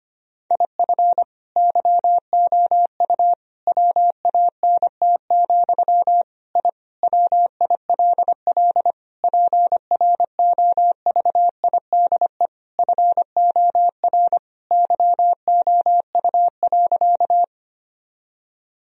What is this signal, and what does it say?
Morse code, heard as a keyed tone, about 25 wpm, 700 Hz. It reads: IF YOU WANT, I WILL PROVIDE FOR YOU.
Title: Sentences from Top 1000 Words - Review Part 3 - 25wpm